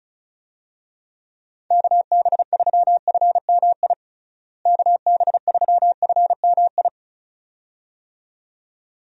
Morse code, heard as a keyed tone, about 36 wpm, 700 Hz. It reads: KB3FMI KB3FMI
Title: USA Call Signs - 6 Characters - Repeated 2x - 35wpm